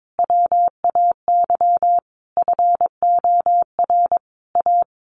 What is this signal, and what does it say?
Morse code, heard as a keyed tone, about 22 wpm, 700 Hz. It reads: WAY FOR A
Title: Sentences from Top 1300 Words 22wpm